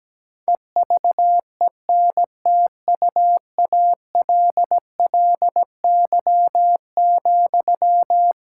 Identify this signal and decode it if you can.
Morse code, heard as a keyed tone, about 17 wpm, 700 Hz. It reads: EVENTUALLY,
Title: Sentences from Top 1000 Words - Review Part 6 - 17wpm